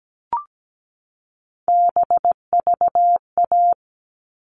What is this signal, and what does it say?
Morse code, heard as a keyed tone, about 17 wpm, 700 Hz. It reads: BVA
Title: All Cut Numbers - 3-Digit 17wpm